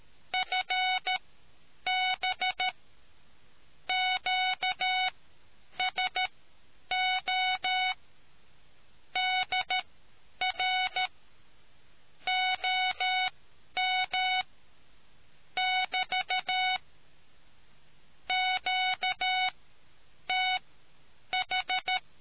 Morse code, heard as a keyed tone, about 13 wpm, spaced as for 8 wpm, 740 Hz. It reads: FB QSO DR OM <BT> QTH